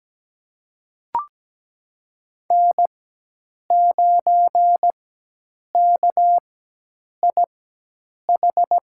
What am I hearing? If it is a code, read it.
Morse code, heard as a keyed tone, about 17 wpm, spaced as for 8 wpm, 700 Hz. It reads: N9KIH
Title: USA Call Signs - 5 Characters - 4x Character Spacing - 17wpm